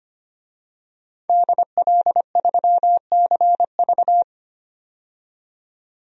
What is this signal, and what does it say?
Morse code, heard as a keyed tone, about 25 wpm, 700 Hz. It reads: DL3CV
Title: DXCC-100 Call Signs - 5 Characters - 25wpm